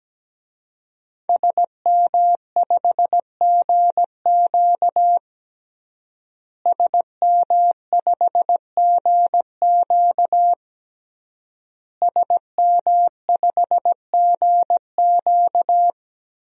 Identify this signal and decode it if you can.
Morse code, heard as a keyed tone, about 17 wpm, 700 Hz. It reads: SM5GQ SM5GQ SM5GQ
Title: DXCC-100 Call Signs - 5 Characters - Repeated 3x - 17wpm